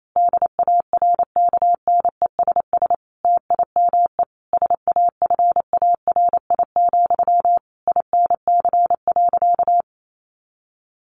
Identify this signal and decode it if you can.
Morse code, heard as a keyed tone, about 28 wpm, 700 Hz. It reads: DARKNESS TIME SAFARI, INC.
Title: A Sound Of Thunder 28wpm